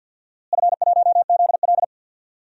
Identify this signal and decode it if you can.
Morse code, heard as a keyed tone, about 53 wpm, 700 Hz. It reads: F1ZL
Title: DXCC-100 Call Signs - 4 Characters - Repeated 2x - 50wpm